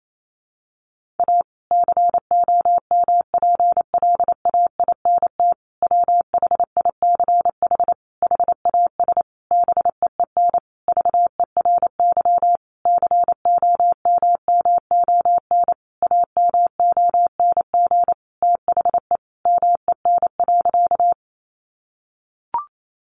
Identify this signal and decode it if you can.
Morse code, heard as a keyed tone, about 28 wpm, 700 Hz. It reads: A COMPLAINT WHICH HAS BEEN VERY COMMON AMONG THE MEN.
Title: Sentences from Top 1400 Words 28wpm